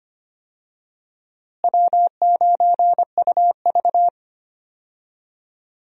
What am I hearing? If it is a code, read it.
Morse code, heard as a keyed tone, about 25 wpm, 700 Hz. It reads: W9UV
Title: USA Call Signs - 4 Characters - Repeated 2x - 25wpm